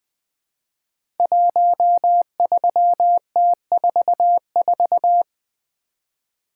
Morse code, heard as a keyed tone, about 20 wpm, 700 Hz. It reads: 13T44
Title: Code Groups - 5 Letter:Numbers - 20wpm